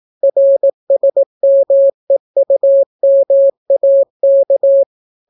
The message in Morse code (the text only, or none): RSMEUMAK